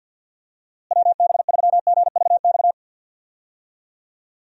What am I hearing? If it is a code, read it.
Morse code, heard as a keyed tone, about 47 wpm, 700 Hz. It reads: WB3GVX
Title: USA Call Signs - 6 Characters - Repeated 2x - 50wpm